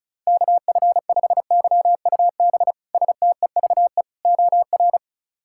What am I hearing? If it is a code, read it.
Morse code, heard as a keyed tone, about 36 wpm, 700 Hz. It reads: KF5YUB STEVE OR
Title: K1USN SST QSO Practice - 35wpm